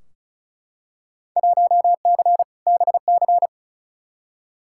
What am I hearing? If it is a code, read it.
Morse code, heard as a keyed tone, about 34 wpm, 700 Hz. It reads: 1C BC
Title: Field Day Exchanges Encore - 35wpm